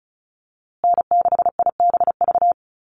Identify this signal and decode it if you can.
Morse code, heard as a keyed tone, about 34 wpm, 700 Hz. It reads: N6IBV
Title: USA Call Signs Repeated 2x 35wpm